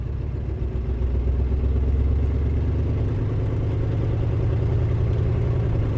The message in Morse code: none